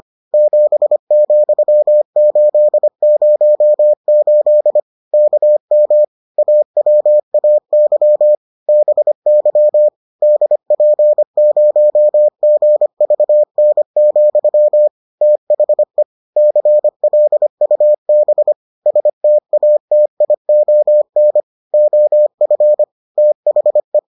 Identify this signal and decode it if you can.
Morse code, heard as a keyed tone, about 25 wpm, 600 Hz. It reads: 7,808 KM AWAY BY DP0GVN, THE CLUB STATION OF THE